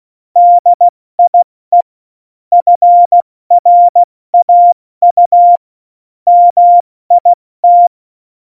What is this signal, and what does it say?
Morse code, heard as a keyed tone, about 16 wpm, spaced as for 14 wpm, 700 Hz. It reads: DIE FRAU MIT
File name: Morsekurs / Witze / Witz 16. Januar 2018, 70 F80